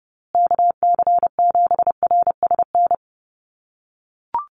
Morse code, heard as a keyed tone, about 30 wpm, 700 Hz. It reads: KC7RSN
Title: USA Call Signs Repeated 2x 30wpm